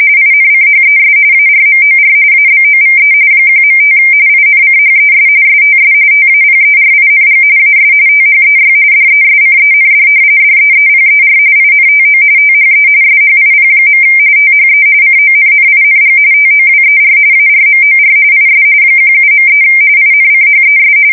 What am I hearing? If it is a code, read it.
RTTY, about 45.45 baud. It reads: INCHES ARMOR ABOVE 1.5 INCHES STS TAPERING AS THEY DESCEND DEEPER INTO THE SHIP. THE CONNING TOWER ARMOR IS 17.3 INCHES
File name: fsk170 / battleshipnj